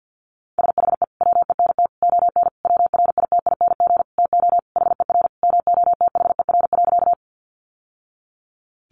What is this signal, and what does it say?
Morse code, heard as a keyed tone, about 50 wpm, 700 Hz. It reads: SHE WENT ON WRITING TO HER MOTHER.
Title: Sentences from Top 800 Words 50wpm